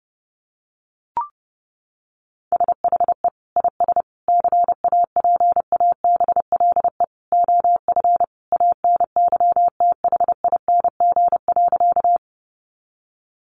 Morse code, heard as a keyed tone, about 30 wpm, 700 Hz. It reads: SHE IS CAPABLE OF ANYTHING.
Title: Sentences from Top 1200 Words 30wpm